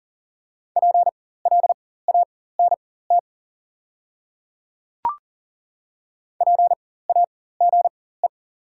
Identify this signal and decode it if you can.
Morse code, heard as a keyed tone, about 40 wpm, spaced as for 19 wpm, 700 Hz. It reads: PLANT PAGE
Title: 4x Character Spacing - Top 300 Words - 40wpm